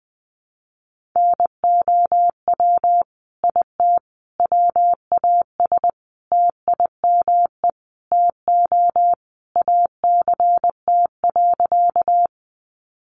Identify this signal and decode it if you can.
Morse code, heard as a keyed tone, about 20 wpm, 700 Hz. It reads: NOW IT WAS TIME TO ACT.
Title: Sentences from Top 200 Words 20wpm